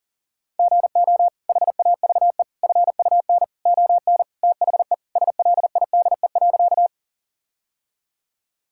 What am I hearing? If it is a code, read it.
Morse code, heard as a keyed tone, about 40 wpm, 700 Hz. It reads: GO HAVE FUN ON THE SLIDE.
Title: Sentences from Top 1600 Words 40wpm